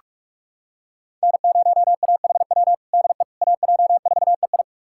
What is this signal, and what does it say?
Morse code, heard as a keyed tone, about 46 wpm, 700 Hz. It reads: N0AHW DE AJ4EI
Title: USA Callsign DE Callsign - 5 Characters - 45wpm